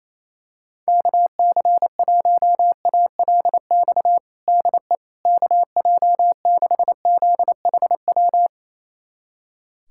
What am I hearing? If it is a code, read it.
Morse code, heard as a keyed tone, about 28 wpm, 700 Hz. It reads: KC1ALX DE KJ6ZHW
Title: USA Callsign DE Callsign - 6 Characters - 28wpm